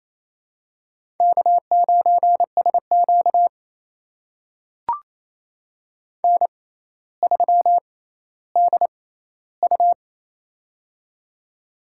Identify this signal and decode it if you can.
Morse code, heard as a keyed tone, about 28 wpm, spaced as for 14 wpm, 700 Hz. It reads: K9SQ N3DU
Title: USA Call Signs - 4 Characters - 6x Character Spacing - 28wpm